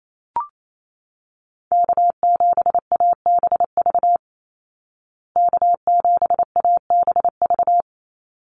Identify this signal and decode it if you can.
Morse code, heard as a keyed tone, about 28 wpm, 700 Hz. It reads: K7ABV K7ABV
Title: Call Signs Repeated 3x - 28wpm